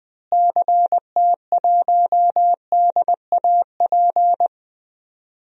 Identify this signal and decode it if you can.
Morse code, heard as a keyed tone, about 20 wpm, 700 Hz. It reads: CT1DAP
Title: DXCC-100 Call Signs - 6 Characters - Repeated 3x - 20wpm